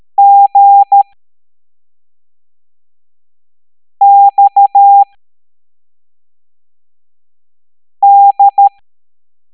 Morse code, heard as a keyed tone, about 13 wpm, spaced as for 3 wpm, 800 Hz. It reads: GXD
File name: hamdata / L03-EX01-IGC